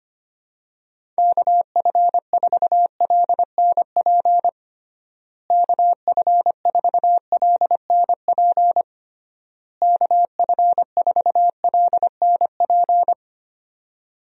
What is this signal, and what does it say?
Morse code, heard as a keyed tone, about 25 wpm, 700 Hz. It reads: KF4LNP KF4LNP KF4LNP
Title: USA Call Signs - 6 Characters - Repeated 3x - 25wpm